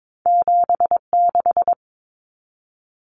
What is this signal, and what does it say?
Morse code, heard as a keyed tone, about 22 wpm, 700 Hz. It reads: 76
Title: Numbers - 2-Digit 22wpm